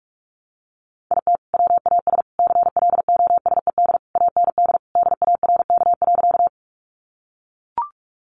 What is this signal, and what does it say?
Morse code, heard as a keyed tone, about 44 wpm, 700 Hz. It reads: IT WAS CLOSED AND DARK.
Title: Sentences from Top 1400 Words 45wpm